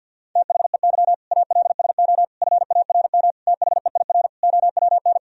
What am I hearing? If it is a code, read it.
Morse code, heard as a keyed tone, about 50 wpm, 700 Hz. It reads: THEY ALSO FARM THEIR OWN
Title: Sentences from Top 1000 Words - Review Part 2 - 50wpm